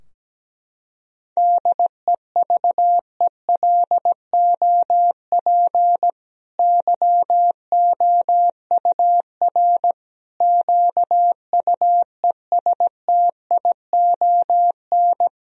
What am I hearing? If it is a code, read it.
Morse code, heard as a keyed tone, about 17 wpm, 700 Hz. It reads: DEVELOP YOUR QUESTION
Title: Sentences from Top 1000 Words - Review Part 2 - 17wpm